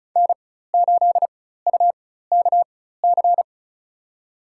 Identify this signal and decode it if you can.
Morse code, heard as a keyed tone, about 34 wpm, spaced as for 16 wpm, 700 Hz. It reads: N8UKC
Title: USA Call Signs - 5 Characters - 4x Character Spacing - 35wpm